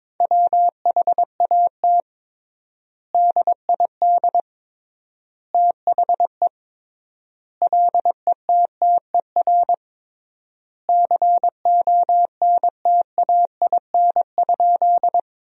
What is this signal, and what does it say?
Morse code, heard as a keyed tone, about 21 wpm, 700 Hz. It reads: WHAT DID THE LETTER CONTAIN?
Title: 3x Word-Spacing - Sentences from Top 500 Words - 22wpm